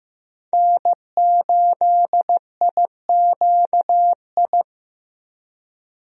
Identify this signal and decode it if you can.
Morse code, heard as a keyed tone, about 15 wpm, 700 Hz. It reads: N8IQI
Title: USA Call Signs - 5 Characters - Repeated 2x - 15wpm